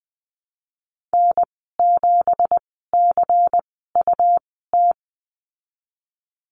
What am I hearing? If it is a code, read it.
Morse code, heard as a keyed tone, about 20 wpm, spaced as for 15 wpm, 700 Hz. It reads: N7CUT
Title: 173 Lesson 28g - Number 9 - Call Signs 20wpm